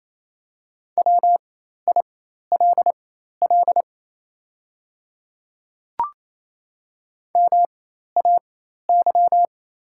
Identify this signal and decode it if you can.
Morse code, heard as a keyed tone, about 28 wpm, spaced as for 13 wpm, 690 Hz. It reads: WILL MAY
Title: 4x Character Spacing - Top 300 Words - 28wpm